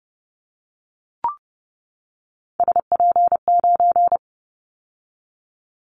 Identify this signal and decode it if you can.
Morse code, heard as a keyed tone, about 30 wpm, 700 Hz. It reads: SP9
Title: Callsign Prefixes for Top 25 DXCC - 30wpm